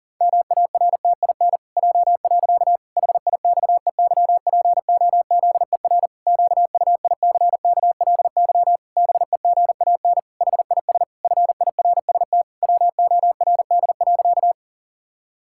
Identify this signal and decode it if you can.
Morse code, heard as a keyed tone, about 37 wpm, 700 Hz. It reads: MARTIN J. HIXEYPOZER QUICKLY BEGAN HIS FIRST WORD.
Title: Pangram Challenge v2 - 40wpm